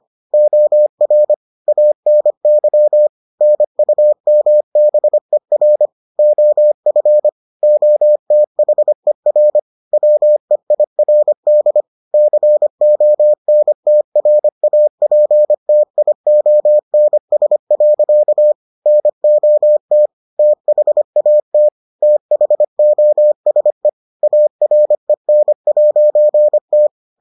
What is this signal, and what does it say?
Morse code, heard as a keyed tone, about 25 wpm, 600 Hz. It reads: OR ANY NUMBER OF OTHER WEIRD CONTRAPTIONS. NOT THAT THOSE AREN'T